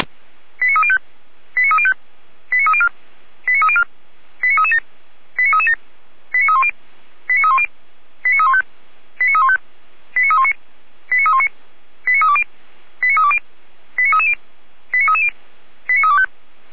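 Five-tone selective call (ZVEI), heard as quick runs of five tones, tones of about 80 ms. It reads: 89305 89305 89304 89304 89337 89337 89310 89310 89315 89315 89319 89319 89320 89320 89330 89330 89325
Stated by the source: Received by SDR, ZVEI five-tone call 89280.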